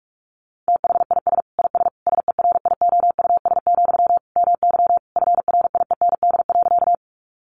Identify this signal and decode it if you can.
Morse code, heard as a keyed tone, about 44 wpm, 700 Hz. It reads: THIS IS SERIOUS, MY FRIEND.
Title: Sentences from Top 1100 Words 45wpm